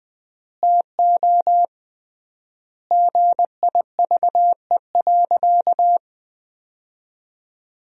Morse code, heard as a keyed tone, about 20 wpm, 700 Hz. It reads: TO GIVE.
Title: 3x Word-Spacing - Sentences from Top 200 Words - 20wpm